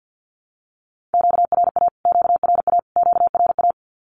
Morse code, heard as a keyed tone, about 48 wpm, 700 Hz. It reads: QRA QRA QRA
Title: Q-Codes Repeated 3x 50wpm